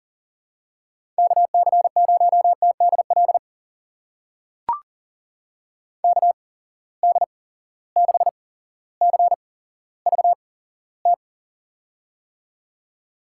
Morse code, heard as a keyed tone, about 40 wpm, spaced as for 17 wpm, 700 Hz. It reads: KC0TDL KD6CVT
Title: USA Call Signs - 8x Character Spacing - 40wpm